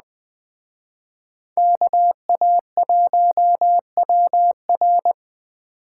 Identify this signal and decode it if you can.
Morse code, heard as a keyed tone, about 20 wpm, 700 Hz. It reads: KA1WR
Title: Speed Racing - USA 2x2 Call Signs 17wpm